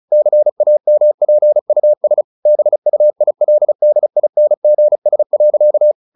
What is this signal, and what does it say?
Morse code, heard as a keyed tone, about 34 wpm, 600 Hz. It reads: CAMPUS BUILDINGS.